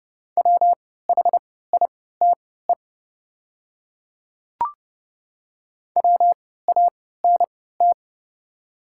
Morse code, heard as a keyed tone, about 31 wpm, spaced as for 17 wpm, 700 Hz. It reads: WHITE WANT
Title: 3x Character Spacing - Top 300 Words - 30wpm